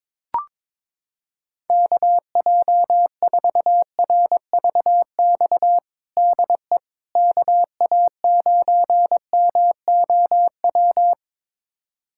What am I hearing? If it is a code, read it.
Morse code, heard as a keyed tone, about 22 wpm, 700 Hz. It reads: KJ4RVX DE KA9MOW
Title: USA Callsign DE Callsign - 6 Characters - 22wpm